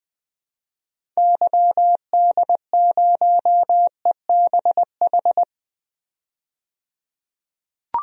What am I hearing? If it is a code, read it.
Morse code, heard as a keyed tone, about 20 wpm, 700 Hz. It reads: YD0EBH
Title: DXCC-100 Call Signs - 6 Characters - 20wpm